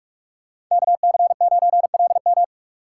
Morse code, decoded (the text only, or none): KC9LM